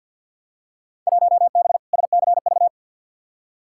Morse code, heard as a keyed tone, about 50 wpm, 700 Hz. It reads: 1B SCV
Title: Field Day Exchanges Encore - 50wpm